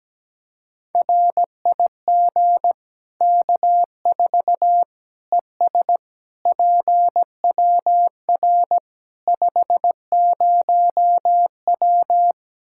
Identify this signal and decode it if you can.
Morse code, heard as a keyed tone, about 17 wpm, 700 Hz. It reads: RIG K4 ES PWR 50W
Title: Practice QSO - Exchange 2 - Info - 17wpm